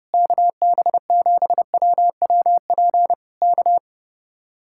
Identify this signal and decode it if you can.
Morse code, heard as a keyed tone, about 28 wpm, 700 Hz. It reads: KB7WWP K
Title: Speed-Racing Common QSO Phrases - 30wpm